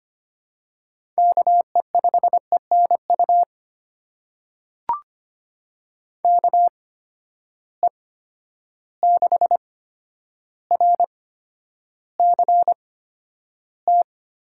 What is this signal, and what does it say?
Morse code, heard as a keyed tone, about 25 wpm, spaced as for 11 wpm, 700 Hz. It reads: KE5ENU KE6RCT
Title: USA Call Signs - 6 Characters - 8x Character Spacing - 25wpm